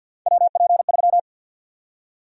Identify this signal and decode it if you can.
Morse code, heard as a keyed tone, about 52 wpm, 700 Hz. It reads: WP3
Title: USA Call Sign Prefixes - Repeated 2x - 50wpm